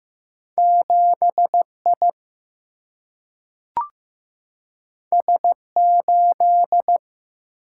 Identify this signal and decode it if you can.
Morse code, heard as a keyed tone, about 15 wpm, 700 Hz. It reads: 7 I S 8
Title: Single-Letter Single-Number 15wpm